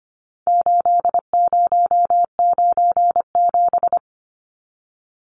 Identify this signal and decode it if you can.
Morse code, heard as a keyed tone, about 25 wpm, 700 Hz. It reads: 8097
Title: Numbers - 4-Digit 25wpm